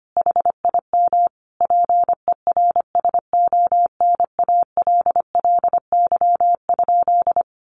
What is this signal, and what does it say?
Morse code, heard as a keyed tone, about 25 wpm, 700 Hz. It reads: HIM PERSONALLY?
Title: Sentences from Top 1400 Words 25wpm